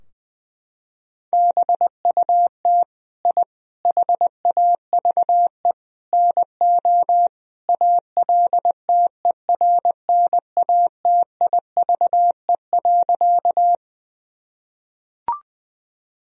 Morse code, heard as a keyed tone, about 20 wpm, 700 Hz. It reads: BUT I HAVE NO ALTERNATIVE.
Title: Sentences from Top 1000 Words - Review Part 4 - 20wpm